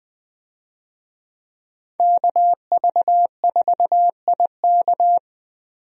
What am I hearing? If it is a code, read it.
Morse code, heard as a keyed tone, about 20 wpm, 700 Hz. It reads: KV4IK
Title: USA Call Signs - 5 Characters - 20wpm